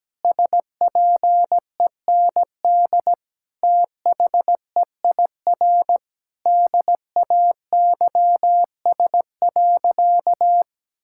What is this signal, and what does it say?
Morse code, heard as a keyed tone, about 17 wpm, 700 Hz. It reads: SPEND THEIR DAYS.
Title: Sentences from Top 1000 Words - Review Part 5 - 17wpm